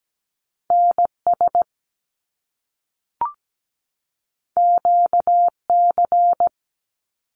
Morse code, heard as a keyed tone, about 17 wpm, 700 Hz. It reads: NS QC
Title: Canadian Provinces and Territories 17wpm